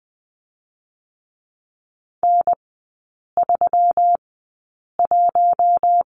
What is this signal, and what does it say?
Morse code, heard as a keyed tone, about 20 wpm, spaced as for 8 wpm, 690 Hz. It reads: N31
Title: 145 Lesson 24e - Letter y - Three Characters 20wpm